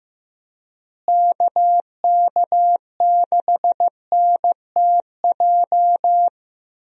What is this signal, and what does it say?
Morse code, heard as a keyed tone, about 15 wpm, 700 Hz. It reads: KK6NTJ
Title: USA Call Signs - 6 Characters - Repeated 3x - 15wpm